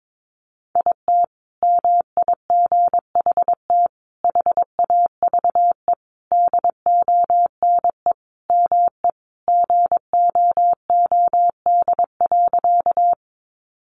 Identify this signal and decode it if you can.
Morse code, heard as a keyed tone, about 22 wpm, 700 Hz. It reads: IT MIGHT HAVE DONE ME GOOD.
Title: Sentences from Top 500 Words 22wpm